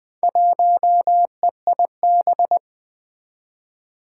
Morse code, heard as a keyed tone, about 20 wpm, 700 Hz. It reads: 1EIB
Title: Four Challenging Letter-Number Combinations 20wpm